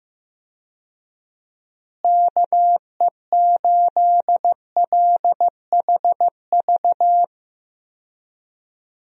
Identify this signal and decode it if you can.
Morse code, heard as a keyed tone, about 15 wpm, 700 Hz. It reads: KE8LHV
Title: USA Call Signs - 6 Characters - Repeated 3x - 15wpm